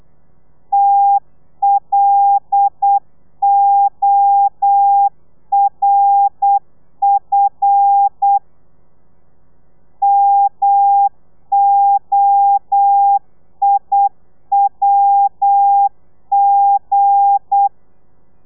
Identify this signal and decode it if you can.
Morse code, heard as a keyed tone, about 8 wpm, 800 Hz. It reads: TLORF MOIWG
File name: Lez11